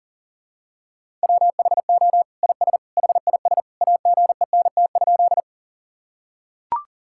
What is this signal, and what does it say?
Morse code, heard as a keyed tone, about 40 wpm, 700 Hz. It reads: WHO IS HIS AGENT?
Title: Sentences from Top 1600 Words 40wpm